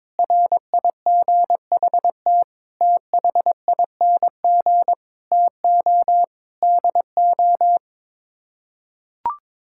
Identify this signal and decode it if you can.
Morse code, heard as a keyed tone, about 21 wpm, 700 Hz. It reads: RIGHT THING TO DO
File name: practice / 22wpm / Build A Phrase - 5-Word Phrases - 22wpm